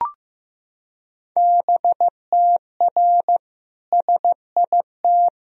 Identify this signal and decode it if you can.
Morse code, heard as a keyed tone, about 15 wpm, 700 Hz. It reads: BTR SIT
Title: Sentences from Top 500 Words and Abbreviations 15wpm